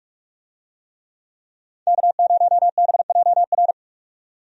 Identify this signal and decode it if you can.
Morse code, heard as a keyed tone, about 47 wpm, 700 Hz. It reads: K0BJR
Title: USA Call Signs - 5 Characters - 45wpm